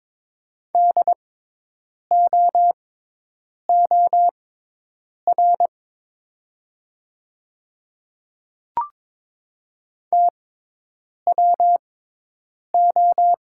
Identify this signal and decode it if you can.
Morse code, heard as a keyed tone, about 22 wpm, spaced as for 8 wpm, 700 Hz. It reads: DOOR TWO
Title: 6x Character Spacing - Top 300 Words - 22wpm